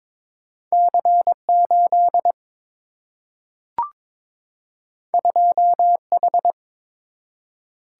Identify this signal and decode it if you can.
Morse code, heard as a keyed tone, about 22 wpm, 700 Hz. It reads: C 8 2 H
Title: Single-Letter Single-Number 22wpm